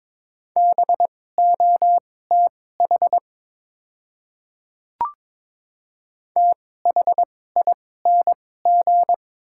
Morse code, heard as a keyed tone, about 22 wpm, spaced as for 16 wpm, 690 Hz. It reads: BOTH THING
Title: 2x Character Spacing - Top 500 Words - 22wpm